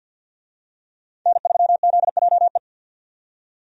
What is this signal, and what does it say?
Morse code, heard as a keyed tone, about 46 wpm, 700 Hz. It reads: N3ZJE
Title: USA Call Signs - 5 Characters - 50wpm